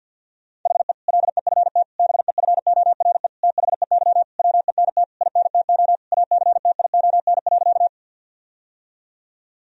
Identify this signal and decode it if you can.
Morse code, heard as a keyed tone, about 50 wpm, 700 Hz. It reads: HE LEFT BEFORE THEY WENT INTO ACTION.
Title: Sentences from Top 1000 Words - Review Part 4 - 50wpm